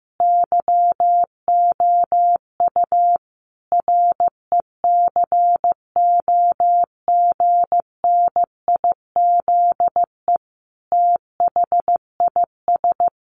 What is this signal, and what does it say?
Morse code, heard as a keyed tone, about 15 wpm, 700 Hz. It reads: YOU RECOGNIZE THIS
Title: Sentences from Top 1400 Words 15wpm